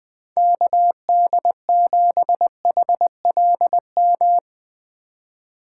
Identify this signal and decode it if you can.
Morse code, heard as a keyed tone, about 20 wpm, 700 Hz. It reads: KD7HLM
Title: USA Call Signs - 6 Characters - 3x Character Spacing - 20wpm